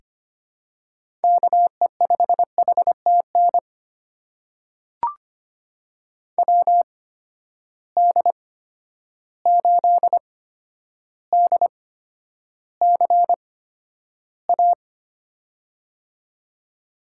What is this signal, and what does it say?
Morse code, heard as a keyed tone, about 25 wpm, spaced as for 11 wpm, 700 Hz. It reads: KE5HTN WD8DCA
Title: USA Call Signs - 6 Characters - 8x Character Spacing - 25wpm